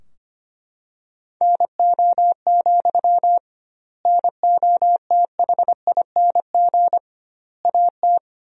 Morse code, heard as a keyed tone, about 25 wpm, 700 Hz. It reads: NO, NOTHING AT
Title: 2x Word-Spacing - Sentences from Top 500 Words - 25wpm